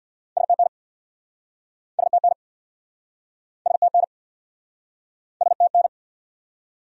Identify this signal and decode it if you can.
Morse code, heard as a keyed tone, about 69 wpm, 700 Hz. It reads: STN STN STN STN